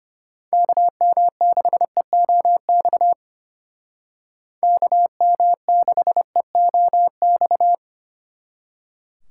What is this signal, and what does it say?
Morse code, heard as a keyed tone, about 30 wpm, 700 Hz. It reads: KM6EOX KM6EOX
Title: Speed Racing - USA 2x3 Call Signs 25wpm